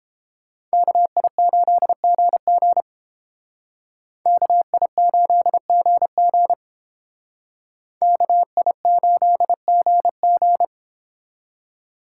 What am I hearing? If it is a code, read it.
Morse code, heard as a keyed tone, about 31 wpm, 700 Hz. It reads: KI8GG KI8GG KI8GG